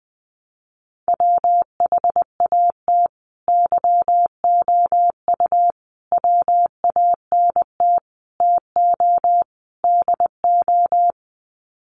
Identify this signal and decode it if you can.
Morse code, heard as a keyed tone, about 20 wpm, 700 Hz. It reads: WHAT YOU WANT TO DO
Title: Sets of 5 Words Encore 20wpm